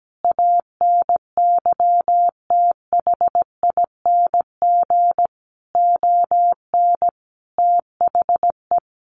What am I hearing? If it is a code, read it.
Morse code, heard as a keyed tone, about 17 wpm, 700 Hz. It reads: ANYTHING ON THE
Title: Sentences from Top 1000 Words 17wpm